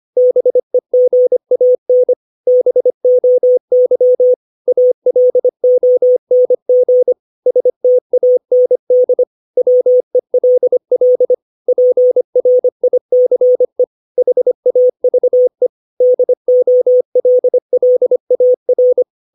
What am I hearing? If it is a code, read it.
Morse code, heard as a keyed tone, about 25 wpm, 500 Hz. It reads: BEGAN BOY ALONG STAND WELL PRICE HAVE DOLLAR
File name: 632 words / 632 Words 25WPM